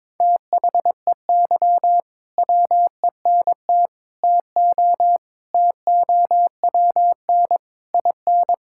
Morse code, heard as a keyed tone, about 22 wpm, 700 Hz. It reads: THEY WENT TO TOWN IN
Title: Sentences from Top 1000 Words - Review Part 5 - 22wpm